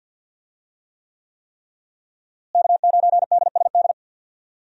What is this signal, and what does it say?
Morse code, heard as a keyed tone, about 50 wpm, 700 Hz. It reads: K9DSD
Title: USA Call Signs - 5 Characters - Repeated 3x - 50wpm